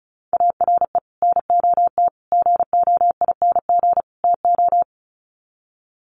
Morse code, heard as a keyed tone, about 34 wpm, 700 Hz. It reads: ARE NOT GOING TO
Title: Sets of 4 Words 35wpm